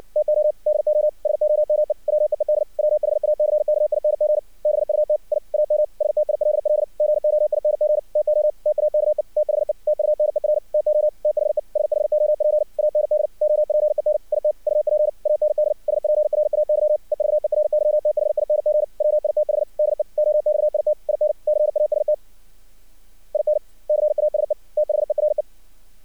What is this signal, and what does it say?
Morse code, heard as a keyed tone, about 60 wpm, 610 Hz. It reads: TO DO SOME WEED WHACKING BUT I AM STILL GOING TO TAKE THE TUNER TO THE SHOP AND OPEN IT UP AND SPRAY EVERYTHING WITH DE OXIT IN CAST IN CASE THERE
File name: archived / CRSnet